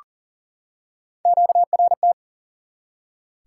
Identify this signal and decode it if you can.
Morse code, heard as a keyed tone, about 40 wpm, 700 Hz. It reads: QRT
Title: Top 180 QSO Elements 40wpm